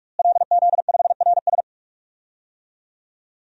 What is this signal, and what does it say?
Morse code, heard as a keyed tone, about 43 wpm, 700 Hz. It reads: LZ5RS
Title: DXCC-100 Call Signs - 5 Characters - Repeated 2x - 45wpm